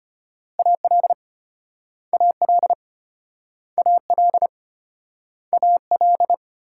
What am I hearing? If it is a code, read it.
Morse code, heard as a keyed tone, about 35 wpm, 700 Hz. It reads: AL AL AL AL